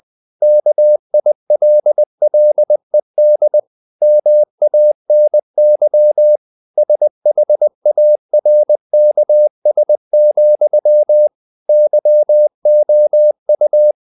Morse code, heard as a keyed tone, about 20 wpm, 600 Hz. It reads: KILLED MANY SHARKS, YOU
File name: the old man and the sea part 2 20wpm